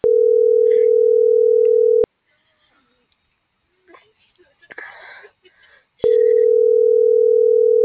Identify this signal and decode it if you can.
Ringback tone (North America), 440 + 480 Hz, with speech.
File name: CENTRAL TERRESTRE / out els..laz.elo.mo.24